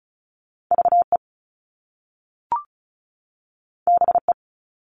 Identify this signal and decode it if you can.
Morse code, heard as a keyed tone, about 35 wpm, 700 Hz. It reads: VE BE
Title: Top Two Letter Combinations in Words 35wpm